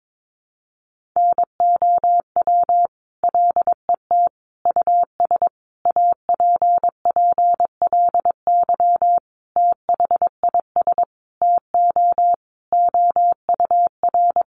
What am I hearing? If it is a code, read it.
Morse code, heard as a keyed tone, about 22 wpm, 700 Hz. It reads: NOW LET US APPLY THIS TO OUR
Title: Sentences from Top 1200 Words 22wpm